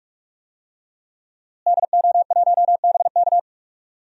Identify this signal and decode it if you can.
Morse code, heard as a keyed tone, about 44 wpm, 700 Hz. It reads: DO1BK
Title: DXCC-100 Call Signs - 5 Characters - Repeated 3x - 45wpm